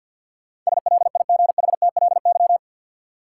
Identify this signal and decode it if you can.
Morse code, heard as a keyed tone, about 52 wpm, 700 Hz. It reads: SLIGHTLY
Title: Sentences from Top 1300 Words - New Word Drill - 50wpm